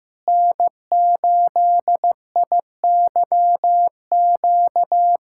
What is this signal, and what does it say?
Morse code, heard as a keyed tone, about 15 wpm, 700 Hz. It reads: N8IYQ
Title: USA Callsign DE Callsign 15wpm